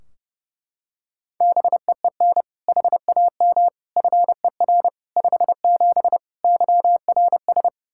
Morse code, heard as a keyed tone, about 30 wpm, 700 Hz. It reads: BEEN HAM FER 57 YRS
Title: K1USN SST QSO Practice 30wpm